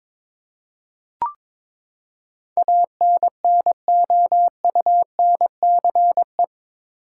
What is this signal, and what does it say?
Morse code, heard as a keyed tone, about 21 wpm, 700 Hz. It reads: ANNOUNCE
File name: practice / 22wpm / Eight Letter Words - 22wpm